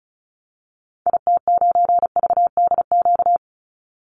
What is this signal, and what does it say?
Morse code, heard as a keyed tone, about 34 wpm, 700 Hz. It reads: IT9VDQ
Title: Call Signs - 35wpm